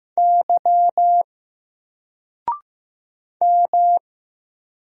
Morse code, heard as a keyed tone, about 15 wpm, 700 Hz. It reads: Y M
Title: Single Letter-Number 15wpm